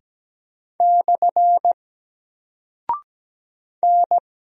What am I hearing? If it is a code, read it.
Morse code, heard as a keyed tone, about 17 wpm, 700 Hz. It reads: / N
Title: Single Character (Letter:Number:Punctuation:Prosign) 17wpm